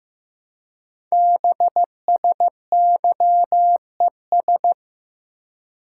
Morse code, heard as a keyed tone, about 15 wpm, 700 Hz. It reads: BSYES
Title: Code Groups - 5 Letters - 15wpm